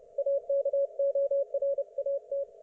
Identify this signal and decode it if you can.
Morse code, heard as a keyed tone, about 30 wpm, 560 Hz. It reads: AKORAT
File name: Archiv / CUC208 / ok1fhi cul208